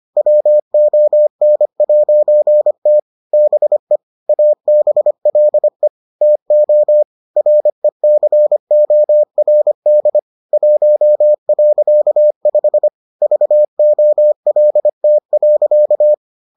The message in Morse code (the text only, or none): WON'T BE ABLE TO RECORD 1.5 VOLT.